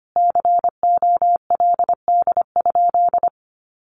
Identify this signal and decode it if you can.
Morse code, heard as a keyed tone, about 25 wpm, 700 Hz. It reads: COLD?
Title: A Sound Of Thunder 25wpm